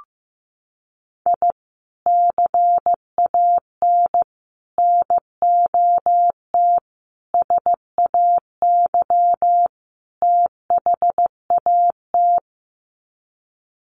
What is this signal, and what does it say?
Morse code, heard as a keyed tone, about 15 wpm, 700 Hz. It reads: I CAN NOT SAY THAT
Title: Sets of 5 Words Encore2 15wpm